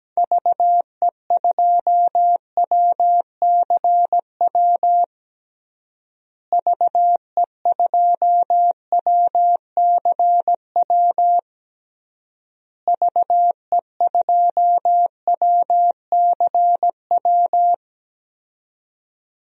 Morse code, heard as a keyed tone, about 17 wpm, 700 Hz. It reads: VE2WCW VE2WCW VE2WCW
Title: DXCC-100 Call Signs - 6 Characters - Repeated 3x - 17wpm